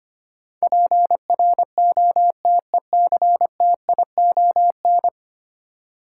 Morse code, heard as a keyed tone, about 25 wpm, 700 Hz. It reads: PROTECTION
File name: 10 Letter Words - First 3 Letters Given - 25wpm